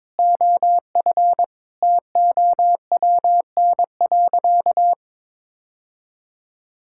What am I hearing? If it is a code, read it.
Morse code, heard as a keyed tone, about 21 wpm, 700 Hz. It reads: OF TOWN.
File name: practice / 22wpm / Sentences with 3-Word Phrases - 22wpm